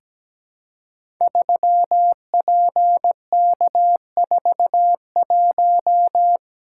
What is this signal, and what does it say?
Morse code, heard as a keyed tone, about 17 wpm, 700 Hz. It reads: 3PK41
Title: Code Groups - 5 Letter:Numbers - 17wpm